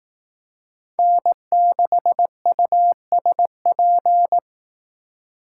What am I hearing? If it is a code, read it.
Morse code, heard as a keyed tone, about 18 wpm, 700 Hz. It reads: N6USP
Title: Speed Racing - USA 1x3 Call Signs 15wpm